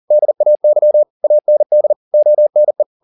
Morse code, heard as a keyed tone, about 40 wpm, 600 Hz. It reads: DAY AND ONE